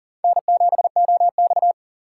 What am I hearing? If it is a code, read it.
Morse code, heard as a keyed tone, about 42 wpm, 700 Hz. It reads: N7OX
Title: USA Call Signs - 4 Characters - 4x Character Spacing - 40wpm